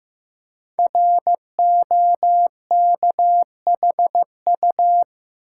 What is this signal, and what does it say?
Morse code, heard as a keyed tone, about 15 wpm, 700 Hz. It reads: ROKHU
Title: Code Groups - 5 Letters - 15wpm